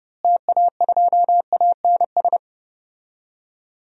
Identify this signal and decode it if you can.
Morse code, heard as a keyed tone, about 32 wpm, 700 Hz. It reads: TA2ANS
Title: DXCC-100 Call Signs - 6 Characters - Repeated 3x - 30wpm